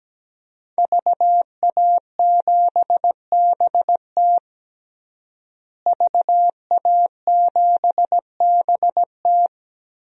Morse code, heard as a keyed tone, about 17 wpm, 700 Hz. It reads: VA7BT VA7BT
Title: DXCC-100 Call Signs - 5 Characters - Repeated 2x - 17wpm